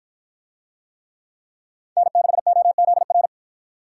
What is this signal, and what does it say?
Morse code, heard as a keyed tone, about 49 wpm, 700 Hz. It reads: N6OZR